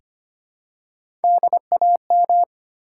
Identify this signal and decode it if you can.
Morse code, heard as a keyed tone, about 25 wpm, 700 Hz. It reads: DAM
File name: practice / 25wpm / All-Age Appropriate Jokes - 25wpm